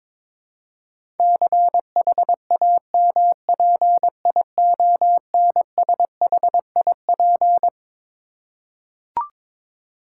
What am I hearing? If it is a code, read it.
Morse code, heard as a keyed tone, about 22 wpm, 700 Hz. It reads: CHAMPIONSHIP
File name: Twelve Letter Words - 22wpm